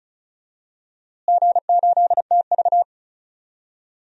Morse code, heard as a keyed tone, about 34 wpm, 700 Hz. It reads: G8TV
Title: DXCC-100 Call Signs - 4 Characters - Repeated 2x - 35wpm